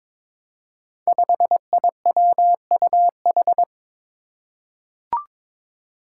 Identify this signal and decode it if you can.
Morse code, heard as a keyed tone, about 22 wpm, 700 Hz. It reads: 5IWUH
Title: Five Challenging Letter-Number Combinations 22wpm